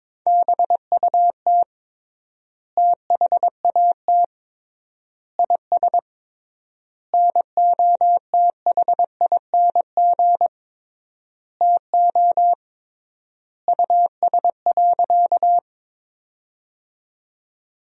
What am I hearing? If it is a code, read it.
Morse code, heard as a keyed tone, about 22 wpm, 700 Hz. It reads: BUT THAT IS NOTHING TO US.
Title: 3x Word-Spacing - Sentences from Top 500 Words - 22wpm